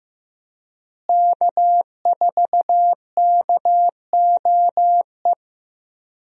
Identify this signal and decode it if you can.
Morse code, heard as a keyed tone, about 15 wpm, 700 Hz. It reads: K4KOE
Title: USA Call Signs - 5 Characters - Repeated 3x - 15wpm